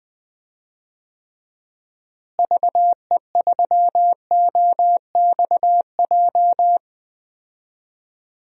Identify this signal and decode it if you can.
Morse code, heard as a keyed tone, about 20 wpm, 700 Hz. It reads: VE3OXJ
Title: DXCC-100 Call Signs - 6 Characters - Repeated 3x - 20wpm